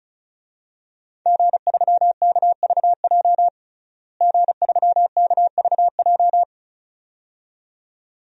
Morse code, heard as a keyed tone, about 36 wpm, 700 Hz. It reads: G3KVJ G3KVJ
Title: DXCC-100 Call Signs - 5 Characters - Repeated 2x - 35wpm